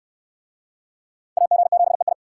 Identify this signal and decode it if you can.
Morse code, heard as a keyed tone, about 64 wpm, 700 Hz. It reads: AG7EI